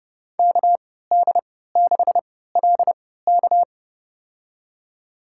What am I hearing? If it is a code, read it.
Morse code, heard as a keyed tone, about 30 wpm, spaced as for 17 wpm, 700 Hz. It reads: KD6LK
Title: USA Call Signs - 5 Characters - 3x Character Spacing - 30wpm